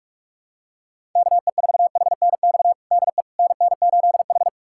Morse code, heard as a keyed tone, about 46 wpm, 700 Hz. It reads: KE4HNX DE NN8H
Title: USA Callsign DE Callsign 45wpm